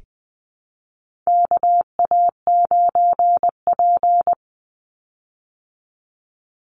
Morse code, heard as a keyed tone, about 20 wpm, 700 Hz. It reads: KA9P
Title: Call Signs Repeated 2x - 20wpm